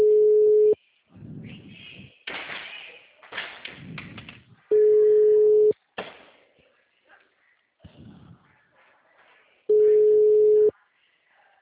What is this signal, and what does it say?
Ringback tone (Europe, 425 Hz), with speech.